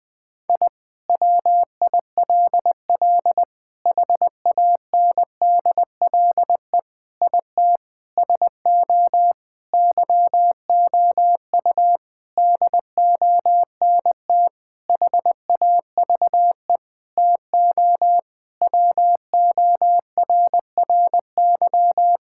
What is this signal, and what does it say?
Morse code, heard as a keyed tone, about 21 wpm, 700 Hz. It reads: I WILL HANDLE IT SO YOU DONT HAVE TO WORRY
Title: Sentences with 7-Word Phrases - 20wpm